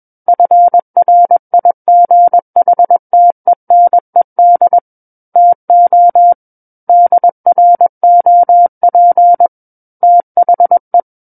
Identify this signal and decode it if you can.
Morse code, heard as a keyed tone, about 21 wpm, 700 Hz. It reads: FRIGHTENED TO DROP THE
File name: Book One / 12-Chapter 12 - 21 WPM